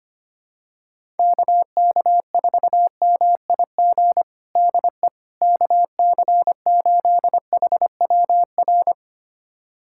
Morse code, heard as a keyed tone, about 25 wpm, 700 Hz. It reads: KK4MIG DE KC8HWR
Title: USA Callsign DE Callsign 25wpm